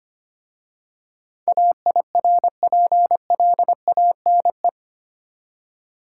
Morse code, heard as a keyed tone, about 25 wpm, 700 Hz. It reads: AIRPLANE
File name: practice / 25wpm / Easy Words with Hints - 25wpm